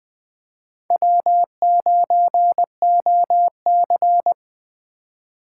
Morse code, heard as a keyed tone, about 20 wpm, 700 Hz. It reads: W9OC
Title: USA Call Signs - 4x Character Spacing - 20wpm